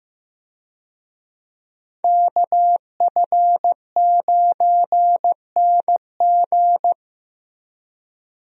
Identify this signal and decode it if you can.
Morse code, heard as a keyed tone, about 15 wpm, 700 Hz. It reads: KF9NG
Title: USA Call Signs - 5 Characters - Repeated 3x - 15wpm